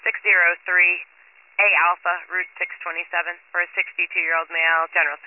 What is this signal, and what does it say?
Emergency Alert System (SAME) tones (header bursts), heard as harsh buzzing bursts, with speech. It is heard from 0.3 to 4.8 s.